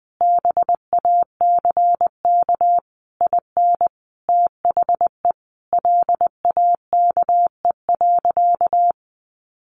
Morse code, heard as a keyed tone, about 20 wpm, 700 Hz. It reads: BACK IN THE LAKE.
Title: Sentences from Top 1100 Words 20wpm